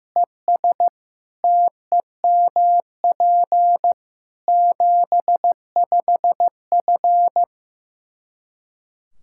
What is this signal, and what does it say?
Morse code, heard as a keyed tone, about 15 wpm, 700 Hz. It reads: ES TEMP 75F
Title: K1USN SST QSO Practice 15wpm